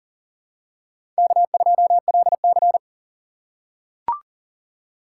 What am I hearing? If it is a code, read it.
Morse code, heard as a keyed tone, about 40 wpm, 700 Hz. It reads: K2LC
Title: USA Call Signs - 4 Characters - 6x Character Spacing - 40wpm